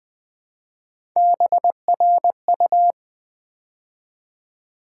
Morse code, heard as a keyed tone, about 20 wpm, 700 Hz. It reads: BRU
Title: Build A Word - 3 Letters At A Time - 6-Letter Words - 20wpm